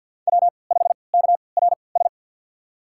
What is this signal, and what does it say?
Morse code, heard as a keyed tone, about 53 wpm, spaced as for 29 wpm, 700 Hz. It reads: W5KRS
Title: USA Call Signs - 5 Characters - 3x Character Spacing - 50wpm